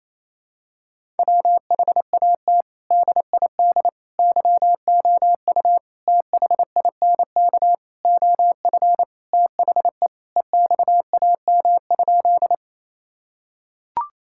Morse code, heard as a keyed tone, about 28 wpm, 700 Hz. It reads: WHAT DID YOU THINK OF THE EXAM?
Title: Sentences from Top 1000 Words - Review Part 5 - 28wpm